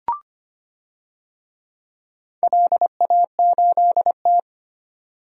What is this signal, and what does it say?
Morse code, heard as a keyed tone, about 25 wpm, 700 Hz. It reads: LA8T
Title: DXCC-100 Call Signs - 4 Characters - 25wpm